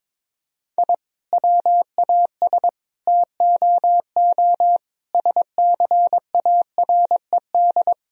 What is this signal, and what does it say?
Morse code, heard as a keyed tone, about 21 wpm, 700 Hz. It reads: I WAS TOO SCARED
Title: Sentences from Top 1000 Words - Review Part 4 - 22wpm